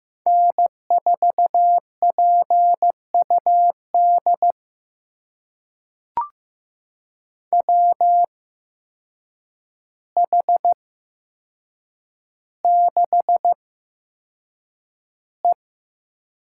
Morse code, heard as a keyed tone, about 15 wpm, spaced as for 7 wpm, 700 Hz. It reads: N4PUD WH6E
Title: USA Call Signs - 8x Character Spacing - 15wpm